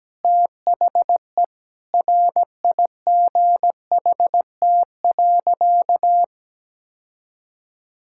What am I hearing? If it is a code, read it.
Morse code, heard as a keyed tone, about 17 wpm, 700 Hz. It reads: THE RIGHT.
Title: Sentences from Top 1600 Words 17wpm